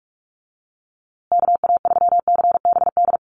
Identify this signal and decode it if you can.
Morse code, heard as a keyed tone, about 44 wpm, 700 Hz. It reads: KA3CBD
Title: USA Call Signs Repeated 3x 45wpm